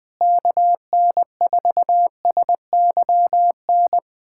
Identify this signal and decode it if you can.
Morse code, heard as a keyed tone, about 20 wpm, 700 Hz. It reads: KN4SYN
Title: USA Callsign DE Callsign - 6 Characters - 20wpm